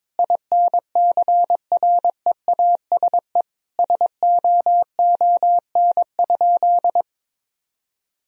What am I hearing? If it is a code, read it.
Morse code, heard as a keyed tone, about 22 wpm, 700 Hz. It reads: INCREASE SOON?
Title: Sentences from Top 1000 Words - Review Part 4 - 22wpm